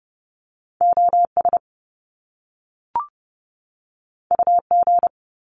Morse code, heard as a keyed tone, about 30 wpm, 700 Hz. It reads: OS UG
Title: Top Two Letter Combinations in Words 30wpm